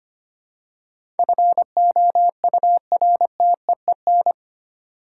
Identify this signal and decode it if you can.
Morse code, heard as a keyed tone, about 25 wpm, 700 Hz. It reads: FOURTEEN
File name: Eight Letter Words - 25wpm